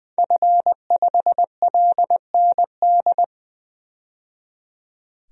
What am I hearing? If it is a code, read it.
Morse code, heard as a keyed tone, about 20 wpm, 700 Hz. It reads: F5LND
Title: DXCC-100 Call Signs - 5 Characters - 20wpm